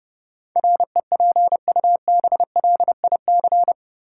Morse code, heard as a keyed tone, about 30 wpm, 700 Hz. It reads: REPUBLIC
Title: Sentences from Top 1500 Words - New Word Drill Alt - 30wpm